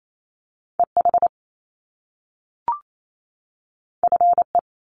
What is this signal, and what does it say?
Morse code, heard as a keyed tone, about 28 wpm, 700 Hz. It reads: EH FE
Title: Top Two Letter Combinations in Words 28wpm